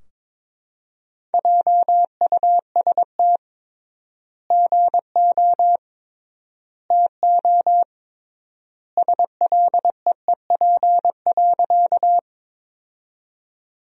Morse code, heard as a keyed tone, about 22 wpm, 700 Hz. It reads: JUST GO TO SLEEP.
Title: 3x Word-Spacing - Sentences from Top 500 Words - 22wpm